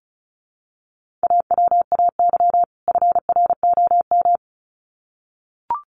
Morse code, heard as a keyed tone, about 34 wpm, 700 Hz. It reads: AWAY FROM
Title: Sets of 2 Words 35wpm